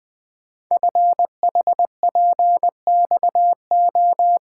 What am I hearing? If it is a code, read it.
Morse code, heard as a keyed tone, about 21 wpm, 700 Hz. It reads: FHPXO
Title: Code Groups - 5 Letters - 20wpm